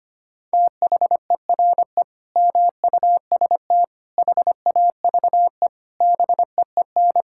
Morse code, heard as a keyed tone, about 25 wpm, 700 Hz. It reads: THERE MUST HAVE BEEN
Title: Sentences from Top 1000 Words - Review Part 6 - 25wpm